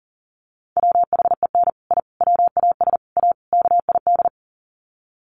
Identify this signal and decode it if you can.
Morse code, heard as a keyed tone, about 40 wpm, 700 Hz. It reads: WHEN I WAS A KID
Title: Sets of 5 Words 40wpm